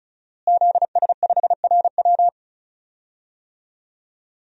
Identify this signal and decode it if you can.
Morse code, heard as a keyed tone, about 36 wpm, 700 Hz. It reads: ZS5RW
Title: DXCC-100 Call Signs - 5 Characters - Repeated 3x - 35wpm